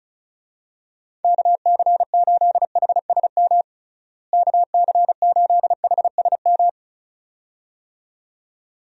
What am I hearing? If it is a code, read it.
Morse code, heard as a keyed tone, about 36 wpm, 700 Hz. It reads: KC8HSM KC8HSM
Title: USA Call Signs - 6 Characters - Repeated 2x - 35wpm